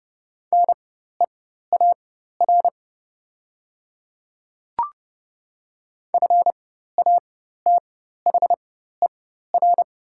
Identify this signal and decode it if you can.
Morse code, heard as a keyed tone, about 30 wpm, spaced as for 14 wpm, 700 Hz. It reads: NEAR FATHER
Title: 4x Character Spacing - Top 500 Words - 30wpm